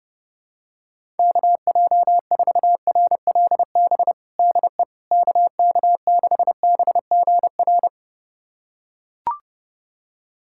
Morse code, heard as a keyed tone, about 30 wpm, 700 Hz. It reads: KJ4RLB DE KK6BGR
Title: USA Callsign DE Callsign - 6 Characters - 30wpm